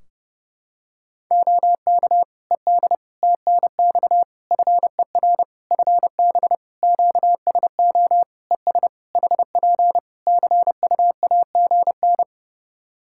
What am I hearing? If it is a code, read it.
Morse code, heard as a keyed tone, about 30 wpm, 700 Hz. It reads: OK ED TNX FER FB QSO ES HP CUAGN
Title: K1USN SST QSO Practice 30wpm